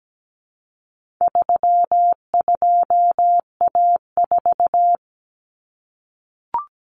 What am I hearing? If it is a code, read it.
Morse code, heard as a keyed tone, about 17 wpm, 700 Hz. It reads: 32A4
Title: Common Cut Numbers - 4-Digit 17wpm